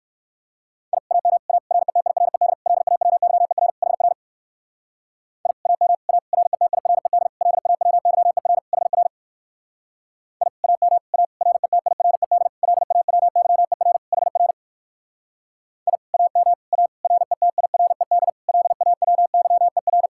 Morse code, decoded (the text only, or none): I AM A RETIRED LAWYER HR I AM A RETIRED LAWYER HR I AM A RETIRED LAWYER HR I AM A RETIRED LAWYER